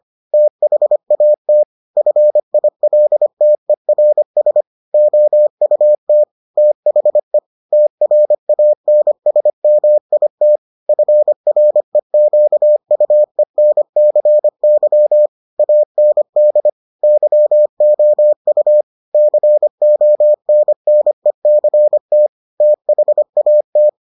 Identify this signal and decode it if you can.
Morse code, heard as a keyed tone, about 25 wpm, 600 Hz. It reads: THAT FILTERS OUT THE TRANSMIT FREQUENCY AND YOU CONNECT THAT